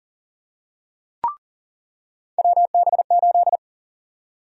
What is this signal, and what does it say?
Morse code, heard as a keyed tone, about 40 wpm, 700 Hz. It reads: WB8
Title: USA Call Sign Prefixes - 40wpm